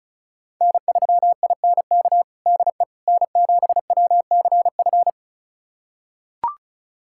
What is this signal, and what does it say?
Morse code, heard as a keyed tone, about 36 wpm, 700 Hz. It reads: N3INK DE N7WCF
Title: USA Callsign DE Callsign - 5 Characters - 35wpm